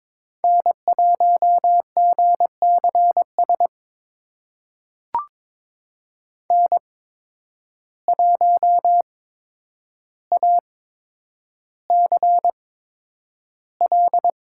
Morse code, heard as a keyed tone, about 22 wpm, spaced as for 9 wpm, 700 Hz. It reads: N1GCS N1ACL
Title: USA Call Signs - 5 Characters - 8x Character Spacing - 22wpm